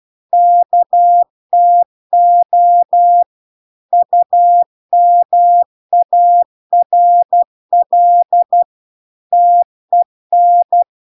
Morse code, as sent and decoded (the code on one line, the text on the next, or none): -.- - --- / ..- -- .- .-. .-.. / - . -.
KTO UMARL TEN